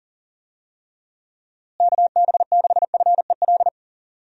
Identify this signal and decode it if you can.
Morse code, heard as a keyed tone, about 40 wpm, 700 Hz. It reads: KB6FEL
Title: USA Call Signs - 6 Characters - Repeated 2x - 40wpm